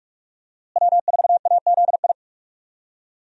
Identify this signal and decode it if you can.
Morse code, heard as a keyed tone, about 46 wpm, 700 Hz. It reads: W4AZI
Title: USA Call Signs - 5 Characters - Repeated 3x - 45wpm